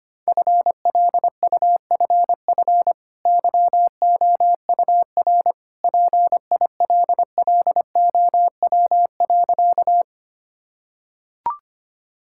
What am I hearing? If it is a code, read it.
Morse code, heard as a keyed tone, about 25 wpm, 700 Hz. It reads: FLUFF YOUR PILLOW.
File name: Easy Sentences with Hints - 25wpm